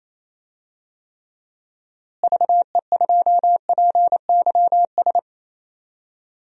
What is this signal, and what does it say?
Morse code, heard as a keyed tone, about 27 wpm, 700 Hz. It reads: VE2PYS
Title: DXCC-100 Call Signs - 6 Characters - Repeated 3x - 28wpm